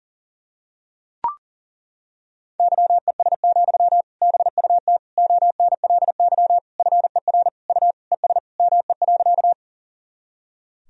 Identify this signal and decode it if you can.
Morse code, heard as a keyed tone, about 40 wpm, 700 Hz. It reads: YES, BUT ONLY FER U ES ME.
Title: Sentences from Top 500 Words and Abbreviations 40wpm